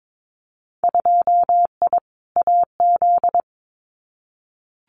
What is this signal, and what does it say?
Morse code, heard as a keyed tone, about 22 wpm, 700 Hz. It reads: 2I AZ
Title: Winter Field Day Exchanges 22wpm